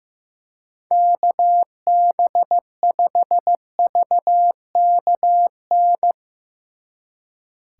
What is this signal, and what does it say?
Morse code, heard as a keyed tone, about 16 wpm, 700 Hz. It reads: KB5VKN
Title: Speed Racing - USA Call Signs 15wpm